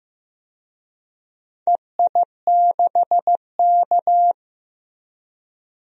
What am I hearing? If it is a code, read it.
Morse code, heard as a keyed tone, about 15 wpm, 700 Hz. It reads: EI6K
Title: DXCC-100 Call Signs - 4 Characters - Repeated 2x - 15wpm